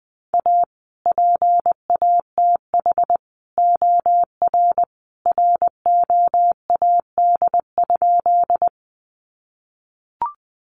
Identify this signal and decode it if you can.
Morse code, heard as a keyed tone, about 19 wpm, 700 Hz. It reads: A PATH OR ROAD?
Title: Sentences from Top 1400 Words 20wpm